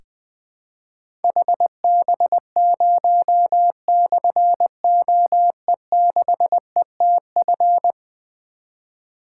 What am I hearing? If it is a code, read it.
Morse code, heard as a keyed tone, about 20 wpm, 700 Hz. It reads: HB0/OE6ETF
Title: Difficult Call Signs 20wpm